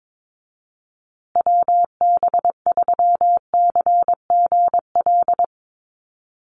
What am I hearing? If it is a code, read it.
Morse code, heard as a keyed tone, about 22 wpm, 700 Hz. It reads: WB3CGL
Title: USA Call Signs Repeated 2x 22wpm